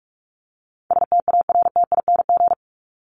Morse code, heard as a keyed tone, about 44 wpm, 700 Hz. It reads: STARTING
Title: ING Suffix - Top 100 Words with Base Words 45wpm